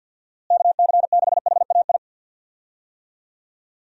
Morse code, heard as a keyed tone, about 52 wpm, 700 Hz. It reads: KC6HAI
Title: USA Call Signs - 6 Characters - Repeated 3x - 50wpm